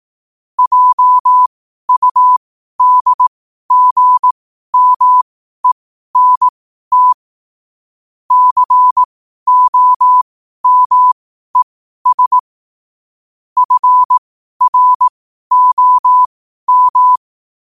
Morse code, heard as a keyed tone, about 18 wpm, spaced as for 12 wpm, 1000 Hz. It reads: JUDGMENT COMES FROM
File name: qotdfiles / 12wpm